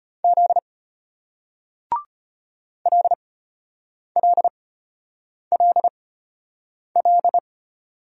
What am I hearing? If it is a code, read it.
Morse code, heard as a keyed tone, about 32 wpm, spaced as for 8 wpm, 700 Hz. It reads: Z LLLL